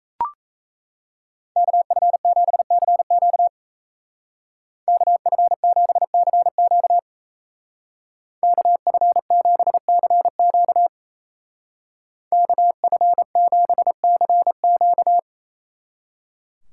Morse code, heard as a keyed tone, about 35 wpm, 700 Hz. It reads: KF7CQ KF7CQ KF7CQ KF7CQ
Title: Speed Racing - USA Call Signs 28wpm